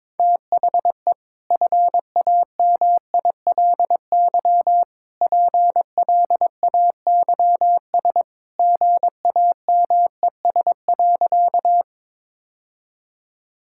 Morse code, heard as a keyed tone, about 22 wpm, 700 Hz. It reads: THE FAMILY PLAYS GAMES.
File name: practice / 22wpm / Easy Sentences with Hints - 22wpm